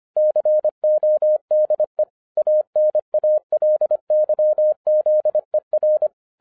Morse code, heard as a keyed tone, about 25 wpm, 610 Hz. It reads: CODE ANALYZER
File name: 2011-10-mp3 / n09-CW TO PS2 CONVERTER 25wpm